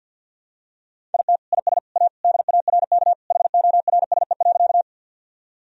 Morse code, heard as a keyed tone, about 51 wpm, 700 Hz. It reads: IT IS A DARK HORSE.
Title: Sentences from Top 1000 Words - Review Part 1 - 50wpm